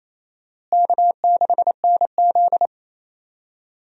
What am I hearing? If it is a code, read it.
Morse code, heard as a keyed tone, about 28 wpm, 700 Hz. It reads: K6NZ
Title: USA Call Signs - 4 Characters - Repeated 3x - 28wpm